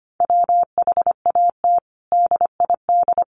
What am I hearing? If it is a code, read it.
Morse code, heard as a keyed tone, about 25 wpm, 700 Hz. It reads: WHAT DID
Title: Sentences from Top 700 Words 25wpm